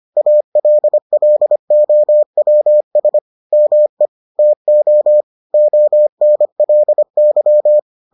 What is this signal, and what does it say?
Morse code, heard as a keyed tone, about 25 wpm, 600 Hz. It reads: ALLOWS ME TO ONLY